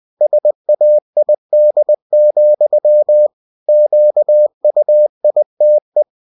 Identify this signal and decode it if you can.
Morse code, heard as a keyed tone, about 20 wpm, 600 Hz. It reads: SAID, QUITE
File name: cw-books / winnie-the-pooh / winnie-20wpm-page-0034